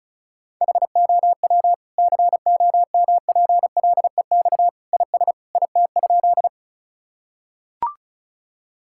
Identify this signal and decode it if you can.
Morse code, heard as a keyed tone, about 36 wpm, 690 Hz. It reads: HOW COMPLEX IS IT?
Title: Sentences from Top 1000 Words - Review Part 6 - 35wpm